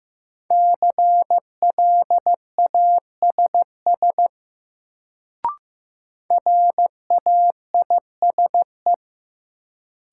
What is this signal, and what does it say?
Morse code, heard as a keyed tone, about 15 wpm, 700 Hz. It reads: CLASS RAISE
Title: Sentences from Top 1400 Words - New Word Drill Alt - 15wpm